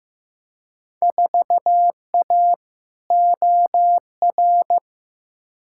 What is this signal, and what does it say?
Morse code, heard as a keyed tone, about 15 wpm, 700 Hz. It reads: 4A OR
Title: Field Day Exchanges - 15wpm